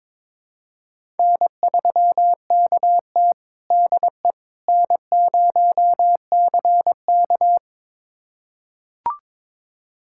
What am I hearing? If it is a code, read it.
Morse code, heard as a keyed tone, about 22 wpm, 700 Hz. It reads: N3KT DE N0CK
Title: USA Callsign DE Callsign - 4 Characters - 22wpm